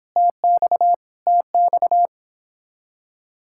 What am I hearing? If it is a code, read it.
Morse code, heard as a keyed tone, about 26 wpm, 700 Hz. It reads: TX TX
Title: Speed-Racing Common QSO Phrases - 22wpm